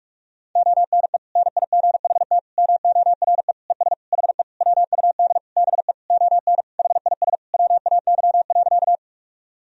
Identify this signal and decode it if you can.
Morse code, heard as a keyed tone, about 46 wpm, 700 Hz. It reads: ONE NIGHT MORE ES HE WUD BE ON HIS WAY.
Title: Sentences from Top 300 Words and Abbreviations 45wpm